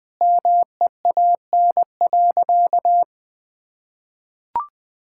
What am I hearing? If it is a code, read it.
Morse code, heard as a keyed tone, about 21 wpm, 700 Hz. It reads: MEAN.
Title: Sentences with 2-Word Phrases - 20wpm